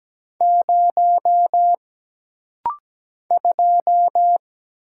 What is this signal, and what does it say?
Morse code, heard as a keyed tone, about 17 wpm, 700 Hz. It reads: 0 2
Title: Single Letter-Number - Rapid-Fire 17wpm